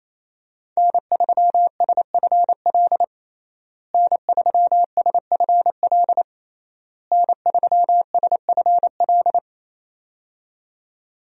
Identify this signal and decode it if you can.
Morse code, heard as a keyed tone, about 28 wpm, 700 Hz. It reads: N3SFL N3SFL N3SFL
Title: USA Call Signs - 5 Characters - Repeated 3x - 28wpm